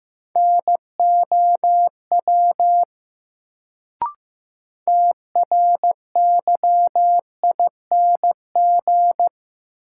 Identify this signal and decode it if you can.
Morse code, heard as a keyed tone, about 15 wpm, 700 Hz. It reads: NOW TRYING
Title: Sets of 2 Words - New Word Drill - 15wpm